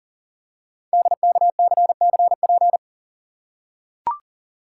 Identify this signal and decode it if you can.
Morse code, heard as a keyed tone, about 40 wpm, 690 Hz. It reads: DKCCP
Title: Code Groups - 5 Letters - 40wpm